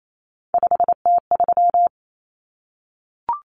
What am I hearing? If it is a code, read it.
Morse code, heard as a keyed tone, about 28 wpm, 700 Hz. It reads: 5T3
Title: Common Cut Numbers - 3-Digit 28wpm